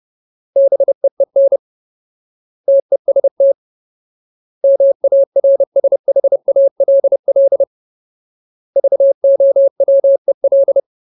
Morse code, heard as a keyed tone, about 30 wpm, 550 Hz. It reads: BEEN TEST MARSHALL VOWEL
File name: Ultimate Word List / 30WPM Ultimate